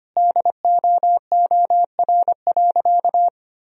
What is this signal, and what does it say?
Morse code, heard as a keyed tone, about 25 wpm, 700 Hz. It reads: DOOR.
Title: Sentences from Top 1000 Words - Review Part 2 - 25wpm